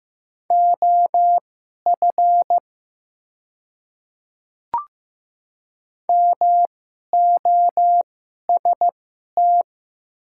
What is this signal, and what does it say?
Morse code, heard as a keyed tone, about 15 wpm, spaced as for 11 wpm, 700 Hz. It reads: OF MOST
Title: 2x Character Spacing - Top 100 Words - 15wpm